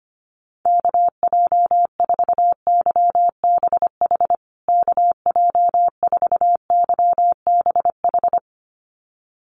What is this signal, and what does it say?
Morse code, heard as a keyed tone, about 25 wpm, 700 Hz. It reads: KJ4YBH KJ4YBH
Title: USA Call Signs Repeated 2x 25wpm